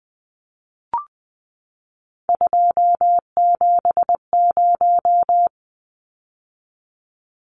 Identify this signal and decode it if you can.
Morse code, heard as a keyed tone, about 20 wpm, 700 Hz. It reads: 270
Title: Numbers - 3-Digit 20wpm